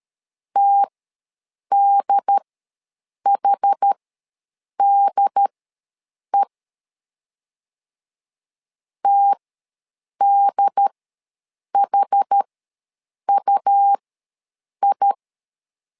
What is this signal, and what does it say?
Morse code, heard as a keyed tone, about 13 wpm, spaced as for 7 wpm, 800 Hz. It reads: TDHDE TDHUI
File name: LEZIONI / 2 UHDN / 2 UHD ws 12wpm 1722